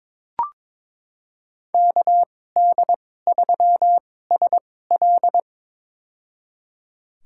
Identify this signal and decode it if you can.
Morse code, heard as a keyed tone, about 22 wpm, spaced as for 16 wpm, 700 Hz. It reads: KD3SL
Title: USA Call Signs - 5 Characters - 2x Character Spacing - 22wpm